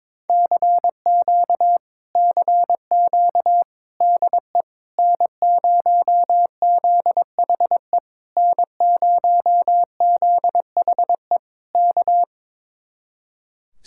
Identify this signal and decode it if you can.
Morse code, heard as a keyed tone, about 21 wpm, 700 Hz. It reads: CQ CQ DE N0ZHE N0ZHE K
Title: Speed-Racing Common QSO Phrases Encore - 22wpm